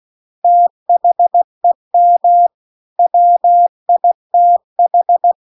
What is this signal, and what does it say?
Morse code, heard as a keyed tone, about 16 wpm, 700 Hz. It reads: THEM WITH
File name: 16WPM